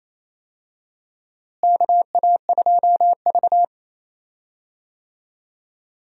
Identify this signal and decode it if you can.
Morse code, heard as a keyed tone, about 28 wpm, 700 Hz. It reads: KA2V
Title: USA Call Signs - 4 Characters - Repeated 3x - 28wpm